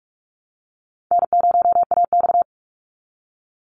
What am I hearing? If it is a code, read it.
Morse code, heard as a keyed tone, about 44 wpm, 700 Hz. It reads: N0AX
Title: Easy Call Signs Repeated 2x - 45wpm